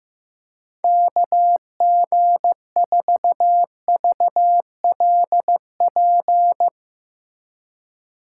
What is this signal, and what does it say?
Morse code, heard as a keyed tone, about 15 wpm, 700 Hz. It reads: KG4VLP
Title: USA Call Signs - 6 Characters - 2x Character Spacing - 15wpm